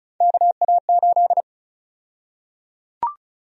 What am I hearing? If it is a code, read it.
Morse code, heard as a keyed tone, about 34 wpm, 700 Hz. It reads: KA8
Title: USA Call Sign Prefixes - 35wpm